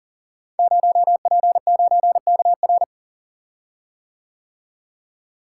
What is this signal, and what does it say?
Morse code, heard as a keyed tone, about 42 wpm, 700 Hz. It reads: 0P9KR
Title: Code Groups - 5 Letter:Numbers - 40wpm